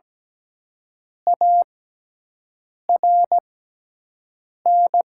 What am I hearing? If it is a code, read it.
Morse code, heard as a keyed tone, about 17 wpm, spaced as for 6 wpm, 700 Hz. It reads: ARN
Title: 6x Character Spacing - Top 500 Words - 17wpm